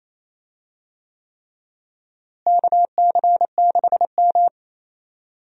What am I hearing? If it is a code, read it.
Morse code, heard as a keyed tone, about 28 wpm, 700 Hz. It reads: KC6M
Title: USA Call Signs - 4 Characters - Repeated 3x - 28wpm